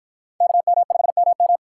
Morse code, heard as a keyed tone, about 49 wpm, 700 Hz. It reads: KM5MM